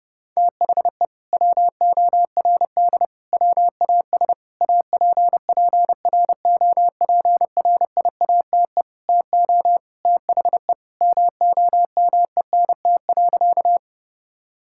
Morse code, heard as a keyed tone, about 28 wpm, 700 Hz. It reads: THE WORD WAS APPROPRIATE TO THE MOMENT.
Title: Sentences from Top 1000 Words - Review Part 5 - 30wpm